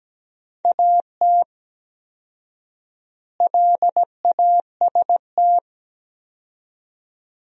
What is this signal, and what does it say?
Morse code, heard as a keyed tone, about 17 wpm, 700 Hz. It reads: AT LAST
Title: 4x Word-Spacing - Sentences from Top 300 Words - 17wpm